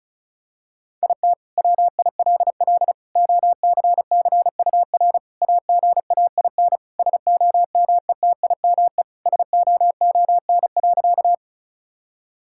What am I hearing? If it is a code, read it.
Morse code, heard as a keyed tone, about 34 wpm, 700 Hz. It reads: IT WILL OCCUR AGAIN SOMETIME SOON.
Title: Sentences from Top 1000 Words - Review Part 6 - 35wpm